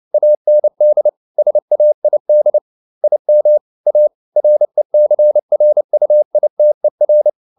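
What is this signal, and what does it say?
Morse code, heard as a keyed tone, about 29 wpm, 600 Hz. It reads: AND SAID IM A RECRUITER
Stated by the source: Morse code at 29 wpm, sending Ken Follett's novel Never in English, from CW med Gnister.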